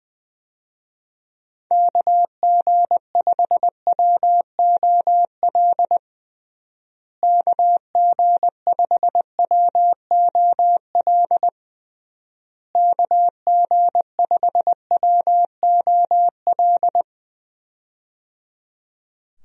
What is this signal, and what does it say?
Morse code, heard as a keyed tone, about 20 wpm, 700 Hz. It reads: KG5WOL KG5WOL KG5WOL
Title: USA Call Signs - 6 Characters - Repeated 3x - 20wpm